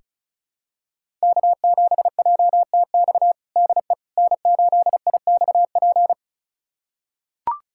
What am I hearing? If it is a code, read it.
Morse code, heard as a keyed tone, about 36 wpm, 700 Hz. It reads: K7JTX DE N8IXP
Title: USA Callsign DE Callsign - 5 Characters - 35wpm